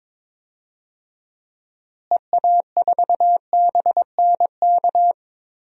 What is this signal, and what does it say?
Morse code, heard as a keyed tone, about 22 wpm, 700 Hz. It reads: EA4BNK
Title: DXCC-100 Call Signs - 6 Characters - Repeated 2x - 22wpm